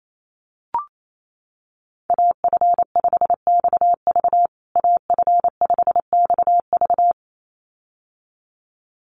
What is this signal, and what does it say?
Morse code, heard as a keyed tone, about 28 wpm, 700 Hz. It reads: AF5XV AF5XV
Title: USA Call Signs Repeated 2x 28wpm